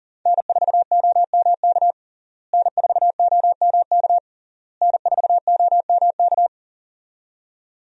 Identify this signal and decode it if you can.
Morse code, heard as a keyed tone, about 40 wpm, 700 Hz. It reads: N4OMK N4OMK N4OMK
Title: USA Call Signs - 5 Characters - Repeated 3x - 40wpm